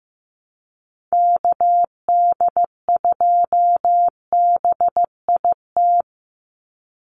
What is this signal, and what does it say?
Morse code, heard as a keyed tone, about 15 wpm, 700 Hz. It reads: KD2BIT
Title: USA Call Signs 15wpm